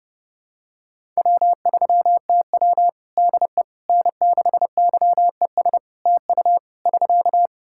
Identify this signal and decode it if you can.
Morse code, heard as a keyed tone, about 30 wpm, 700 Hz. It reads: W3TW DE N6YES TU <SK>
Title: Practice QSO - Exchange 4 - Ending - 30wpm